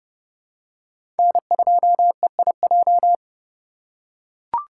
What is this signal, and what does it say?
Morse code, heard as a keyed tone, about 31 wpm, 700 Hz. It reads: N2EIJ
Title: USA Call Signs - 5 Characters - 3x Character Spacing - 30wpm